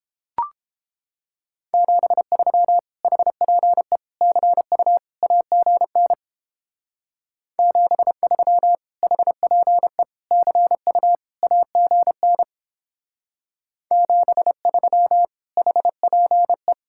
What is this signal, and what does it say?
Morse code, heard as a keyed tone, about 30 wpm, 700 Hz. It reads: 73 HPE CU AGN 73 HPE CU AGN 73 HPE